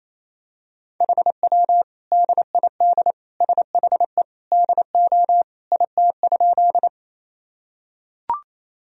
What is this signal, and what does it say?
Morse code, heard as a keyed tone, about 28 wpm, 700 Hz. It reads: HW DID SHE DO IT?
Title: Sentences from Top 100 Words and Abbreviations 28wpm